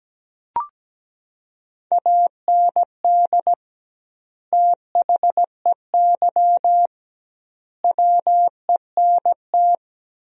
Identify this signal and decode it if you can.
Morse code, heard as a keyed tone, about 17 wpm, 700 Hz. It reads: AND THEY WENT
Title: 2x Word-Spacing - Sentences from Top 200 Words - 17wpm